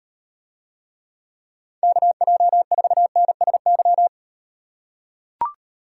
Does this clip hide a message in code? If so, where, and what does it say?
Yes: Morse code from 1.8 to 4.1 s, heard as a keyed tone, about 38 wpm, 700 Hz. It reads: KJ4NSY